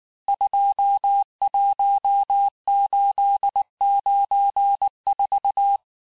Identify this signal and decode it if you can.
Morse code, heard as a keyed tone, about 19 wpm, 800 Hz. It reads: 21894
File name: cours-du-lundi files / cours-1595-teste-CHIFFRE-vitesse18-MPM